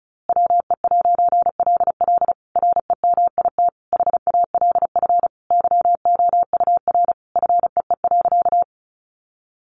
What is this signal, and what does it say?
Morse code, heard as a keyed tone, about 34 wpm, 700 Hz. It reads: WE'LL REMIT HALF YOUR FEE.
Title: A Sound Of Thunder 35wpm